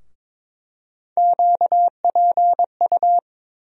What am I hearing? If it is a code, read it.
Morse code, heard as a keyed tone, about 22 wpm, 700 Hz. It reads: QPU
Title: USA Call Sign Postfixes - 3 Letters 22wpm